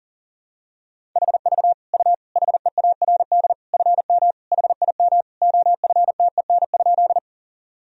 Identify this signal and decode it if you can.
Morse code, heard as a keyed tone, about 37 wpm, 700 Hz. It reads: HV U HEARD FM HIM OFTEN?
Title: Sentences from Top 400 Words and Abbreviations 40wpm